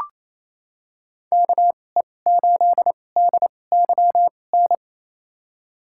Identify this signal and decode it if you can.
Morse code, heard as a keyed tone, about 28 wpm, spaced as for 20 wpm, 700 Hz. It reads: KE8DYN
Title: USA Call Signs - 6 Characters - 2x Character Spacing - 28wpm